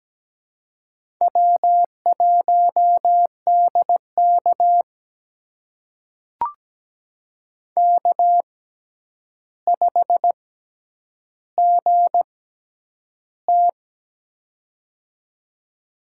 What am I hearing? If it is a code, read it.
Morse code, heard as a keyed tone, about 17 wpm, spaced as for 9 wpm, 700 Hz. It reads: W1DK K5GT
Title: USA Call Signs - 6x Character Spacing - 17wpm